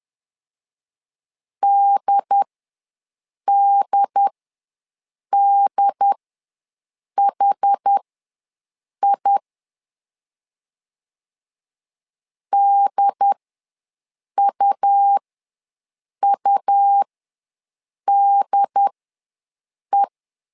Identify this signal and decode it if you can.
Morse code, heard as a keyed tone, about 10 wpm, spaced as for 6 wpm, 800 Hz. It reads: DDDHI DUUDE
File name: LEZIONI / 2 UHDN / 2 UHD ws 10wpm 2058